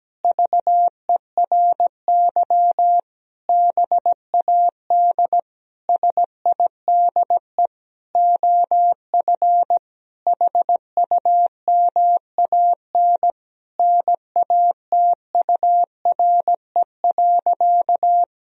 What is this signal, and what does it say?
Morse code, heard as a keyed tone, about 17 wpm, 700 Hz. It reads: VERY BAD SIDE OF HUMAN NATURE.
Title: Sentences from Top 1000 Words - Review Part 3 - 17wpm